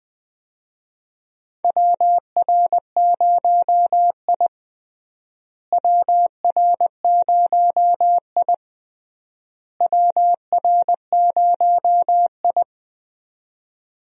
Morse code, heard as a keyed tone, about 20 wpm, 700 Hz. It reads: WR0I WR0I WR0I
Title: USA Call Signs - 4 Characters - Repeated 3x - 20wpm